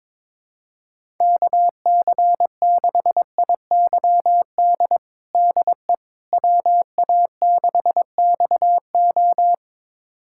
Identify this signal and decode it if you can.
Morse code, heard as a keyed tone, about 22 wpm, 700 Hz. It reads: KC6IYD DE WA6XO
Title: USA Callsign DE Callsign 22wpm